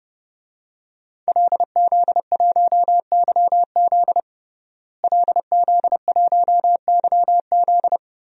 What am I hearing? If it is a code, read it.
Morse code, heard as a keyed tone, about 32 wpm, 700 Hz. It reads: LZ1YZ LZ1YZ
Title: DXCC-100 Call Signs - 5 Characters - Repeated 2x - 30wpm